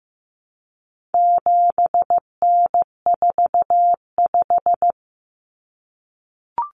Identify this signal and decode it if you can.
Morse code, heard as a keyed tone, about 14 wpm, 700 Hz. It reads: 7N45
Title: Common Cut Numbers - 4-Digit 15wpm